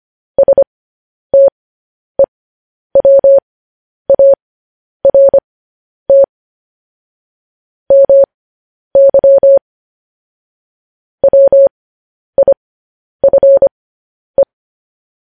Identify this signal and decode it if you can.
Morse code, heard as a keyed tone, about 25 wpm, spaced as for 10 wpm, 550 Hz. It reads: STEWART MY WIFE
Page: news headlines in morse code at 10 wpm